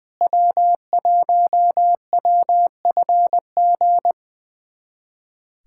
Morse code, heard as a keyed tone, about 20 wpm, 700 Hz. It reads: W1WFG
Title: USA Callsign DE Callsign - 5 Characters - 20wpm